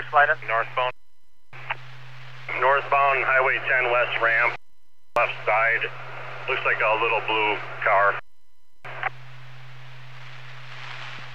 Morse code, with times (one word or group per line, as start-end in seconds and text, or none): none